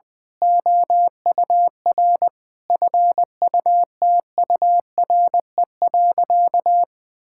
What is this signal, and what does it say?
Morse code, heard as a keyed tone, about 20 wpm, 700 Hz. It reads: OUR FUTURE.
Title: Sentences from Top 1000 Words - Review Part 6 - 20wpm